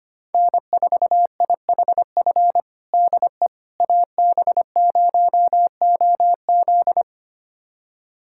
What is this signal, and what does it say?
Morse code, heard as a keyed tone, about 25 wpm, 700 Hz. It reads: N4IHF DE AB0OZ
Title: USA Callsign DE Callsign - 5 Characters - 25wpm